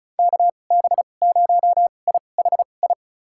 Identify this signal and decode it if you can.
Morse code, heard as a keyed tone, about 34 wpm, spaced as for 25 wpm, 700 Hz. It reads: KB0IHI
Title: USA Call Signs - 2x Character Spacing - 35wpm